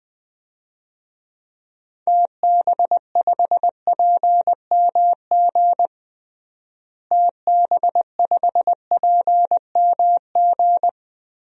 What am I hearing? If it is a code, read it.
Morse code, heard as a keyed tone, about 20 wpm, 700 Hz. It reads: TB5PMG TB5PMG
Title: DXCC-100 Call Signs - 6 Characters - Repeated 3x - 20wpm